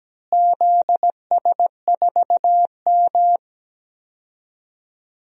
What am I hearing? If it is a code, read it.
Morse code, heard as a keyed tone, about 17 wpm, 700 Hz. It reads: ZS4M
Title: DXCC-100 Call Signs - 4 Characters - Repeated 2x - 17wpm